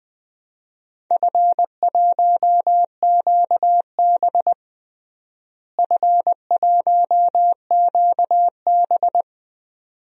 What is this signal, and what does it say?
Morse code, heard as a keyed tone, about 20 wpm, 700 Hz. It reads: F1QB F1QB
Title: DXCC-100 Call Signs - 4 Characters - Repeated 2x - 20wpm